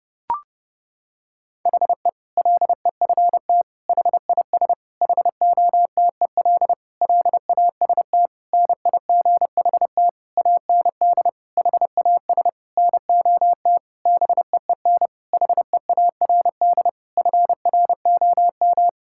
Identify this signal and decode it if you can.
Morse code, heard as a keyed tone, about 30 wpm, 700 Hz. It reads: HE LEFT HIS HOTEL LAST NIGHT AND HAS NOT BEEN HEARD FROM
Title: Sentences from Top 1600 Words 30wpm